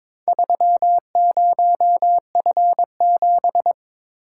Morse code, heard as a keyed tone, about 22 wpm, 700 Hz. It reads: 30F7
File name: Four Letter-Number Combinations - 22wpm